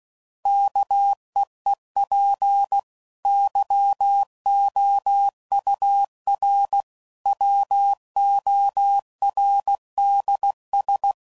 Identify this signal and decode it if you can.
Morse code, heard as a keyed tone, about 16 wpm, 800 Hz. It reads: KEEP YOUR WORDS